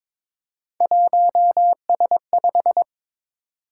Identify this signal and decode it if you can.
Morse code, heard as a keyed tone, about 23 wpm, 700 Hz. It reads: 1S5
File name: practice / 22wpm / Three Letter-Number Combinations - 22wpm